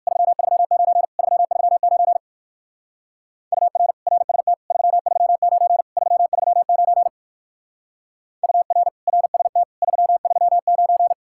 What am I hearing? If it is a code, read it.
Morse code, heard as a keyed tone, about 59 wpm, 700 Hz. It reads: 339 339 UR RST 339 339 UR RST 339